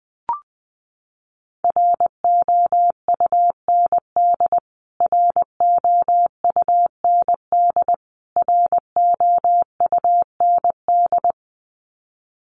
Morse code, heard as a keyed tone, about 20 wpm, 700 Hz. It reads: ROUND ROUND ROUND
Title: Top 500 Words Repeated 20wpm